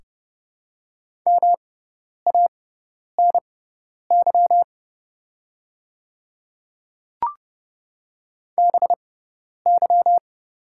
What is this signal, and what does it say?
Morse code, heard as a keyed tone, about 30 wpm, spaced as for 10 wpm, 690 Hz. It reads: MANY BY
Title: 6x Character Spacing - Top 100 Words - 30wpm